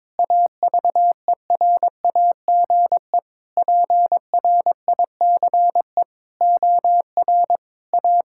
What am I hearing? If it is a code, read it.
Morse code, heard as a keyed tone, about 22 wpm, 700 Hz. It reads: AVERAGE PRICE OR A
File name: Sets of 9 Words - 22wpm